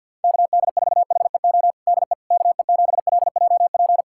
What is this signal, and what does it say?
Morse code, heard as a keyed tone, about 51 wpm, 700 Hz. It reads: KD4HEO DE KE7LJP
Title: USA Callsign DE Callsign 50wpm